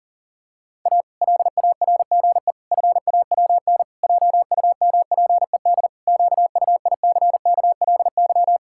Morse code, heard as a keyed tone, about 40 wpm, 700 Hz. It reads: A LARGE FAWN JUMPED QUICKLY
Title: Pangram Challenge v2 - 40wpm